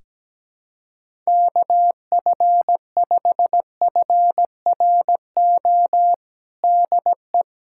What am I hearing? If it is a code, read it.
Morse code, heard as a keyed tone, about 17 wpm, 700 Hz. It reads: KF5FRO DE
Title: USA Callsign DE Callsign - 6 Characters - 17wpm